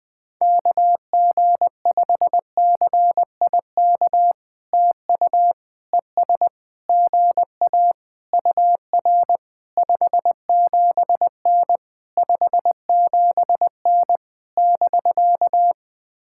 Morse code, heard as a keyed tone, about 20 wpm, 700 Hz. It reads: KG5CIK TU ES GA UR 57N 57N <BK>
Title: POTA QSO Practice - 20wpm